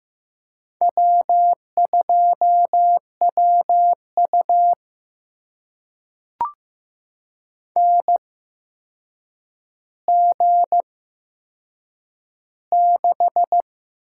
Morse code, heard as a keyed tone, about 15 wpm, spaced as for 7 wpm, 700 Hz. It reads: W2WU NG6
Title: USA Call Signs - 8x Character Spacing - 15wpm